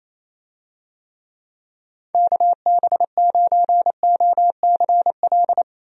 Morse code, heard as a keyed tone, about 28 wpm, 700 Hz. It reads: KB9OCL
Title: USA Call Signs - 6 Characters - Repeated 3x - 28wpm